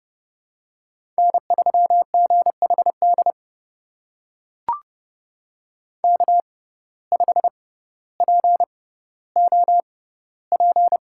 Morse code, heard as a keyed tone, about 31 wpm, spaced as for 16 wpm, 700 Hz. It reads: N3GHD K5POP
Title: USA Call Signs - 5 Characters - 6x Character Spacing - 30wpm